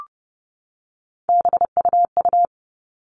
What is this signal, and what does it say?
Morse code, heard as a keyed tone, about 30 wpm, 700 Hz. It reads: BUU
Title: All Cut Numbers - 3-Digit 30wpm